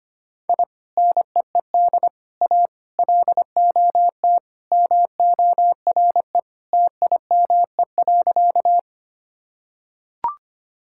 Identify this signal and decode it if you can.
Morse code, heard as a keyed tone, about 25 wpm, 700 Hz. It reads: I NEED A LOT MORE TIME.
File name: practice / 25wpm / Sentences with 3-Word Phrases - 25wpm